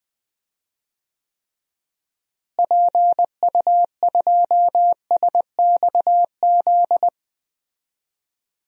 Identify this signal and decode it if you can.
Morse code, heard as a keyed tone, about 20 wpm, 700 Hz. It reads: PU2SXZ
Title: DXCC-100 Call Signs - 6 Characters - Repeated 2x - 20wpm